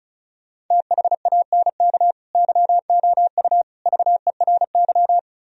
Morse code, heard as a keyed tone, about 47 wpm, spaced as for 41 wpm, 700 Hz. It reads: THANK YOU VERY
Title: Speed-Racing - Sets of 5 Words 35wpm